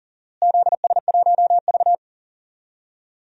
Morse code, heard as a keyed tone, about 40 wpm, 700 Hz. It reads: ZS1V
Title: DXCC-100 Call Signs - 4 Characters - Repeated 2x - 40wpm